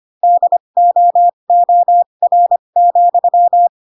Morse code, heard as a keyed tone, about 25 wpm, spaced as for 22 wpm, 700 Hz. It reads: DOOR,
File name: Pinocchio / Pinocchio ch7 18 25 ik1hge